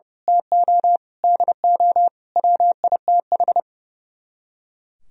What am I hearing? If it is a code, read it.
Morse code, heard as a keyed tone, about 33 wpm, 700 Hz. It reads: TO DO WITH
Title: Speed-Racing - Sets of 5 Words 30wpm